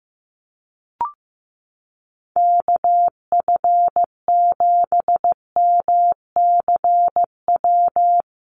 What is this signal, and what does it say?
Morse code, heard as a keyed tone, about 14 wpm, 700 Hz. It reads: KF7MCW
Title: USA Call Signs Repeated 2x 15wpm